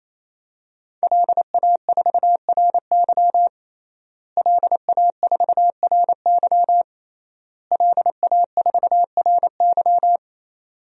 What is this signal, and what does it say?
Morse code, heard as a keyed tone, about 27 wpm, 700 Hz. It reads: LA4RY LA4RY LA4RY
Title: DXCC-100 Call Signs - 5 Characters - Repeated 3x - 28wpm